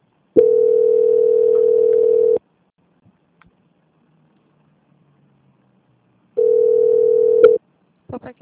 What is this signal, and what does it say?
Ringback tone (North America), 440 + 480 Hz, with speech.
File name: out- -2039- - - .10